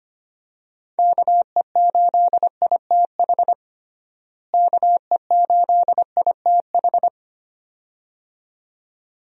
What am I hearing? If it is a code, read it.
Morse code, heard as a keyed tone, about 25 wpm, 700 Hz. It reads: KE8ITH KE8ITH
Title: USA Call Signs - 6 Characters - Repeated 2x - 25wpm